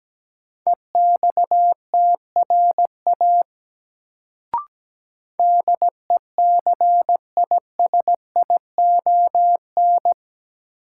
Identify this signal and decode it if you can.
Morse code, heard as a keyed tone, about 17 wpm, 700 Hz. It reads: EXTRA DECISION
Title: Sentences from Top 1000 Words - New Word Drill - 17wpm